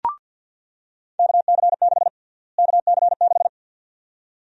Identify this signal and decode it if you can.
Morse code, heard as a keyed tone, about 48 wpm, 700 Hz. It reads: KC6 KC6
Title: USA Call Sign Prefixes - Repeated 2x - 50wpm